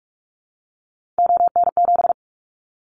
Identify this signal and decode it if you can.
Morse code, heard as a keyed tone, about 46 wpm, 690 Hz. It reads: ON7
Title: Callsign Prefixes for Top 50 DXCC - 45wpm